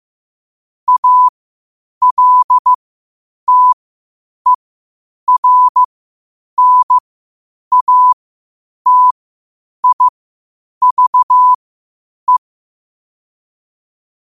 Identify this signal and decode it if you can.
Morse code, heard as a keyed tone, about 15 wpm, spaced as for 8 wpm, 1000 Hz. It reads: ALTERNATIVE
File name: qotdfiles / 8wpm